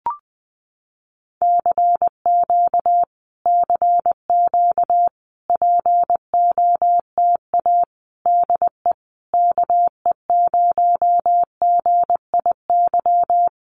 Morse code, heard as a keyed tone, about 20 wpm, 700 Hz. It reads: CQ CQ POTA DE KE0GIY
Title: Common QSO Phrases Encore 20wpm